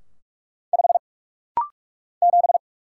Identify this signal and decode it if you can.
Morse code, heard as a keyed tone, about 46 wpm, 700 Hz. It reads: 5 7
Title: Single Letter-Number - Rapid-Fire 45wpm